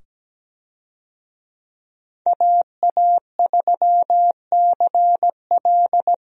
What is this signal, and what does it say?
Morse code, heard as a keyed tone, about 17 wpm, 700 Hz. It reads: AA3CL
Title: USA Call Signs - 5 Characters - 17wpm